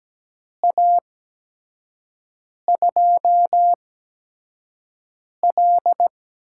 Morse code, heard as a keyed tone, about 17 wpm, spaced as for 5 wpm, 700 Hz. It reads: A2L
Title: USA Call Signs - 6 Characters - 8x Character Spacing - 17wpm